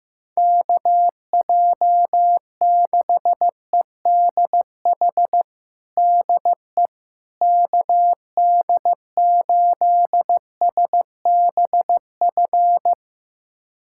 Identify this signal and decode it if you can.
Morse code, heard as a keyed tone, about 16 wpm, 700 Hz. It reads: KJ6EDH DE KD8SBF
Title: USA Callsign DE Callsign - 6 Characters - 15wpm